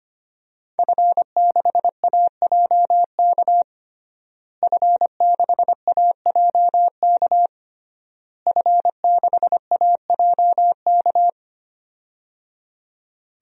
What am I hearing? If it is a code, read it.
Morse code, heard as a keyed tone, about 25 wpm, 700 Hz. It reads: F6AJK F6AJK F6AJK
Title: DXCC-100 Call Signs - 5 Characters - Repeated 3x - 25wpm